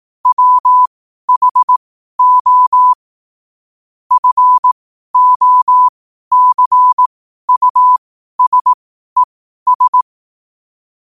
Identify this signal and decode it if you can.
Morse code, heard as a keyed tone, about 18 wpm, spaced as for 12 wpm, 1000 Hz. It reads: WHO FOCUSES
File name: qotdfiles / 12wpm